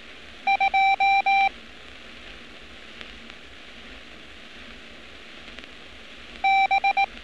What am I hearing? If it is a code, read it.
Morse code, heard as a keyed tone, about 18 wpm, 780 Hz. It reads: 2 B
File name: training / NAVPERS 11200-11211 / NAVPERS 11206 RA RB